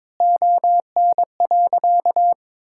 Morse code, heard as a keyed tone, about 22 wpm, 700 Hz. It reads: ON.
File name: Sentences with 9-Word Phrases - 22wpm